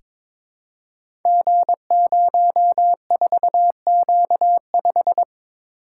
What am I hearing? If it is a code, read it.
Morse code, heard as a keyed tone, about 22 wpm, 700 Hz. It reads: G04Q5
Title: Code Groups - 5 Letter:Numbers - 22wpm